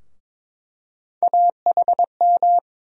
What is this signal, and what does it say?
Morse code, heard as a keyed tone, about 21 wpm, 700 Hz. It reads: AHM
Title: USA Call Sign Postfixes - 3 Letters 22wpm